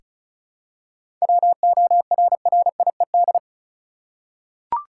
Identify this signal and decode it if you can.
Morse code, heard as a keyed tone, about 36 wpm, 700 Hz. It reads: WORRIED
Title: Seven Letter Words - 35wpm